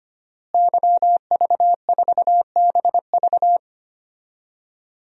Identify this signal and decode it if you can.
Morse code, heard as a keyed tone, about 25 wpm, 700 Hz. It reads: YV4BV
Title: DXCC-100 Call Signs - 5 Characters - Repeated 3x - 25wpm